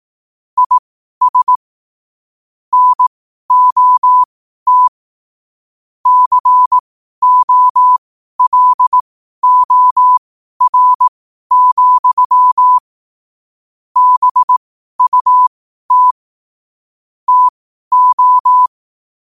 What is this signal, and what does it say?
Morse code, heard as a keyed tone, about 18 wpm, spaced as for 12 wpm, 1000 Hz. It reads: IS NOT COLOR, BUT TO
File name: qotdfiles / 12wpm